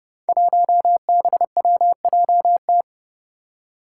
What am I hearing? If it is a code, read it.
Morse code, heard as a keyed tone, about 29 wpm, 700 Hz. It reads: 1BWJT
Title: Five Challenging Letter-Number Combinations 30wpm